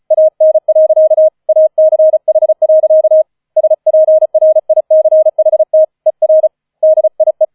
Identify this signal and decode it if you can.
Morse code, heard as a keyed tone, about 34 wpm, 630 Hz. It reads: AN. ACH. SPRICHT ER DIE
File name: 004 Vierter Streich 35wpm